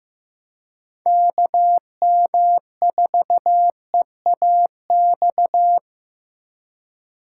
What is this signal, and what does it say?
Morse code, heard as a keyed tone, about 15 wpm, 700 Hz. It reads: KM4EAX
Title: USA Call Signs - 6 Characters - 4x Character Spacing - 15wpm